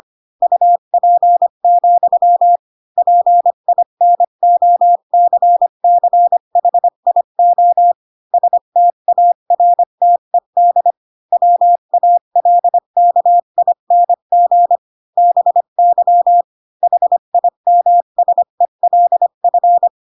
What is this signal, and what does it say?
Morse code, heard as a keyed tone, about 25 wpm, 700 Hz. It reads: UP, PINOCCHIO STARTED WALKING BY HIMSELF
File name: Pinocchio / Pinocchio ch3 20 25 ik1hge